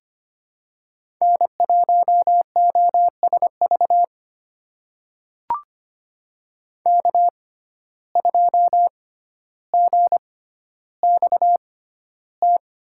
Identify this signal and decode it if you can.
Morse code, heard as a keyed tone, about 25 wpm, spaced as for 13 wpm, 700 Hz. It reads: N1OSV K2GXT
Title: USA Call Signs - 5 Characters - 6x Character Spacing - 25wpm